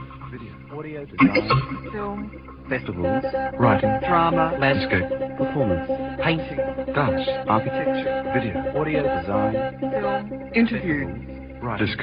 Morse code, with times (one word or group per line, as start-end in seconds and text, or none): none